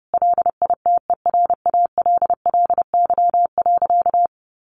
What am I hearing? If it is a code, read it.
Morse code, heard as a keyed tone, about 30 wpm, 700 Hz. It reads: LITERALLY.
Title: Sentences from Top 1400 Words 30wpm